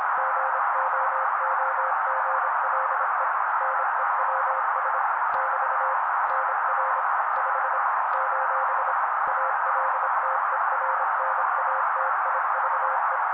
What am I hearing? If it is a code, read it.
Morse code, heard as a keyed tone, about 26 wpm, 530 Hz. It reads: GOOGLE NEWS <BT> NR 5 : ALTERNATIVE
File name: daily / 25wpm noise